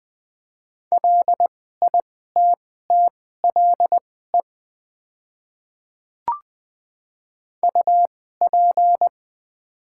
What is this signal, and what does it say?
Morse code, heard as a keyed tone, about 20 wpm, spaced as for 14 wpm, 700 Hz. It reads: LITTLE UP
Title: 2x Character Spacing - Top 200 Words - 20wpm